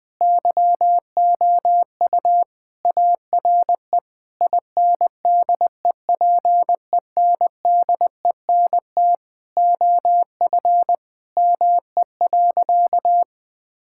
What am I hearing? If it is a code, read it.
Morse code, heard as a keyed tone, about 21 wpm, 700 Hz. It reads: YOU ARE INDEPENDENT OF ME.
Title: Sentences from Top 1000 Words - Review Part 6 - 20wpm